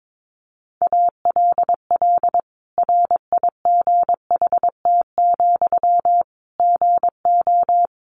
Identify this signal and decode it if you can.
Morse code, heard as a keyed tone, about 22 wpm, 700 Hz. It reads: ALL RIGHT, GO
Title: Sentences from Top 1100 Words 22wpm